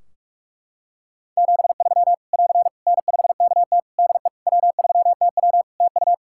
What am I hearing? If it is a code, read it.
Morse code, heard as a keyed tone, about 44 wpm, 700 Hz. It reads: 73 <AR> N5KT DE W3TW TU
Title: Practice QSO - Exchange 4 - Ending - 45wpm